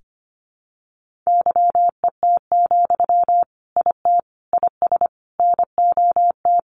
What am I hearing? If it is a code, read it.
Morse code, heard as a keyed tone, about 25 wpm, 700 Hz. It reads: YET, IT IS NOT
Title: Sentences from Top 1400 Words 25wpm